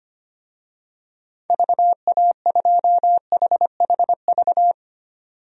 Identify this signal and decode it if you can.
Morse code, heard as a keyed tone, about 25 wpm, 700 Hz. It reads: VA2HHV
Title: DXCC-100 Call Signs - 6 Characters - 25wpm